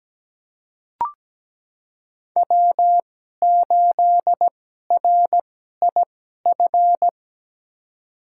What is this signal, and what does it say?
Morse code, heard as a keyed tone, about 17 wpm, spaced as for 12 wpm, 700 Hz. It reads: W8RIF
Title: USA Call Signs - 5 Characters - 2x Character Spacing - 17wpm